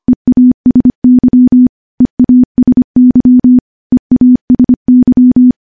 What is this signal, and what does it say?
Morse code, heard as a keyed tone, about 25 wpm, 250 Hz. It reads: EASY EASY EASY
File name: morse / morse-training-25WPM